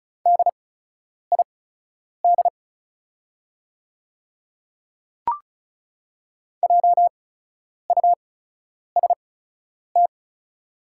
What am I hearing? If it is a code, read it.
Morse code, heard as a keyed tone, about 36 wpm, spaced as for 10 wpm, 700 Hz. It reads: DID JUST
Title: 8x Character Spacing - Top 200 Words - 35wpm